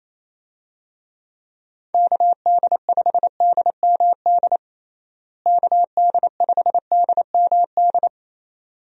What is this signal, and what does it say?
Morse code, heard as a keyed tone, about 28 wpm, 700 Hz. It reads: KD5DMD KD5DMD
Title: USA Call Signs - 6 Characters - Repeated 2x - 28wpm